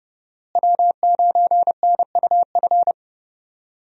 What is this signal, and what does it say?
Morse code, heard as a keyed tone, about 32 wpm, 700 Hz. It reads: W9NUF
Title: USA Call Signs - 5 Characters - Repeated 3x - 30wpm